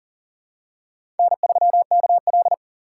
Morse code, heard as a keyed tone, about 40 wpm, 700 Hz. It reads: N3KL
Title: USA Call Signs - 3x Character Spacing - 40wpm